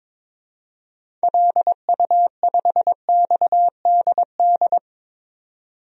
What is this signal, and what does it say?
Morse code, heard as a keyed tone, about 22 wpm, 700 Hz. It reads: LU5XDD
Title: DXCC-100 Call Signs - 6 Characters - 22wpm